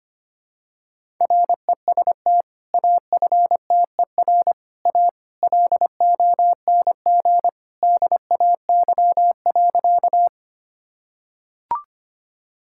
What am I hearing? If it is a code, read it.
Morse code, heard as a keyed tone, about 25 wpm, 700 Hz. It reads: REST AFTER A LONG DAY.
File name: practice / 25wpm / Sentences with 2-Word Phrases - 25wpm